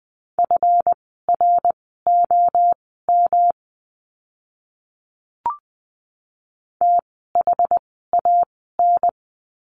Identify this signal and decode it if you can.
Morse code, heard as a keyed tone, about 20 wpm, spaced as for 14 wpm, 700 Hz. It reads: FROM THAN
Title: 139 Lesson 23f - Letter f - Words 20wpm